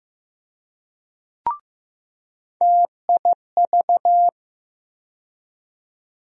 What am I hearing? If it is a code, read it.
Morse code, heard as a keyed tone, about 15 wpm, 700 Hz. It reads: TIV
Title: USA Call Sign Postfixes - 3 Letters 15wpm